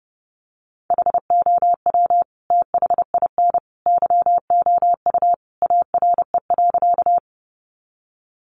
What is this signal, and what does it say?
Morse code, heard as a keyed tone, about 30 wpm, 700 Hz. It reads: HOW THIN YOU ARE.
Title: Sentences from Top 1200 Words 30wpm